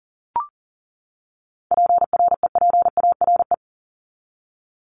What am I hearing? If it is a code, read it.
Morse code, heard as a keyed tone, about 40 wpm, 700 Hz. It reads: PREPARE
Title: ED Suffix - Top 100 Words with Base Words 40wpm